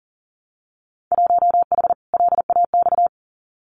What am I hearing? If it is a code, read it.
Morse code, heard as a keyed tone, about 42 wpm, 700 Hz. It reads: 1H LAX
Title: Winter Field Day Exchanges 40wpm